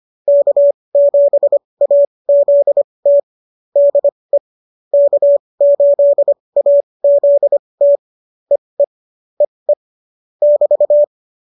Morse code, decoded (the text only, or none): K7AZT DE K8AZT EE EE <BT>